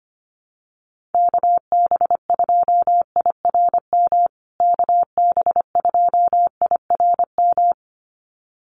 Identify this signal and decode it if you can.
Morse code, heard as a keyed tone, about 25 wpm, 700 Hz. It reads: KB2IRM KB2IRM
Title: USA Call Signs Repeated 2x 25wpm